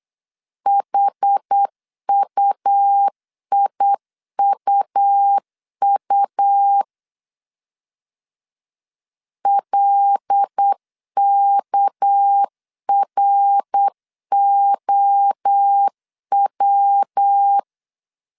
Morse code, heard as a keyed tone, about 8 wpm, 800 Hz. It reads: HUIUU LKROW
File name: CW / DET / 1 6 1500C S B 8wpm 5142